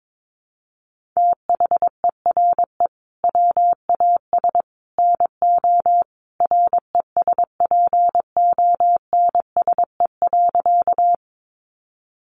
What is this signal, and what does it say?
Morse code, heard as a keyed tone, about 22 wpm, 700 Hz. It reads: THERE WAS NO RESPONSE.
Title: Sentences from Top 1400 Words 22wpm